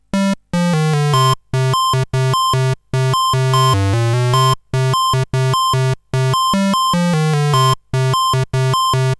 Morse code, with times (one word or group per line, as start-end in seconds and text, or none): none